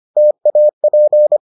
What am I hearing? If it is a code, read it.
Morse code, heard as a keyed tone, about 25 wpm, 600 Hz. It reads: TAP